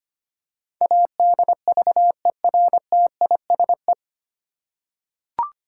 Morse code, heard as a keyed tone, about 25 wpm, 700 Hz. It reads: ADVERTISE
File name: Nine Letter Words - 25wpm